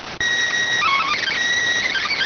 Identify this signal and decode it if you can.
SSTV (slow-scan television), PD 290.